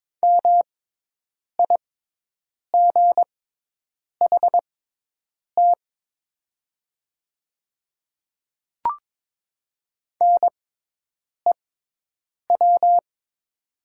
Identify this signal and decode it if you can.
Morse code, heard as a keyed tone, about 22 wpm, spaced as for 8 wpm, 700 Hz. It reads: MIGHT NEW
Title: 6x Character Spacing - Top 300 Words - 22wpm